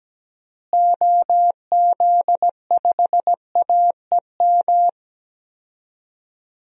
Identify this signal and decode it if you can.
Morse code, heard as a keyed tone, about 17 wpm, 700 Hz. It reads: OZ5AEM
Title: DXCC-100 Call Signs - 6 Characters - Repeated 3x - 17wpm